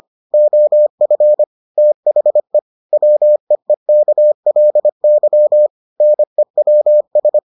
Morse code, heard as a keyed tone, about 25 wpm, 600 Hz. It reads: OF THE WEEKLY NEWS